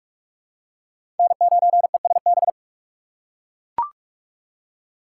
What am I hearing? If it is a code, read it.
Morse code, heard as a keyed tone, about 44 wpm, 700 Hz. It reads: N9ESB
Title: USA Call Signs - 5 Characters - 6x Character Spacing - 45wpm